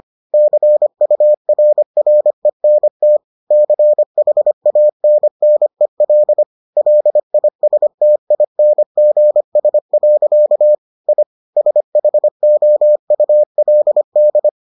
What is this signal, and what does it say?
Morse code, heard as a keyed tone, about 25 wpm, 600 Hz. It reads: CURRENT CHANNEL LISTINGS. I SHOULD